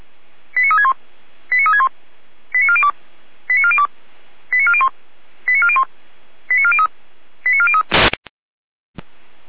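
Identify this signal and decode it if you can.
Five-tone selective call (ZVEI), heard as quick runs of five tones, tones of about 70 ms. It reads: 89361 89361 89402 89402 89401 89401 89403 89403
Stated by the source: ZVEI five-tone selective call received by SDR.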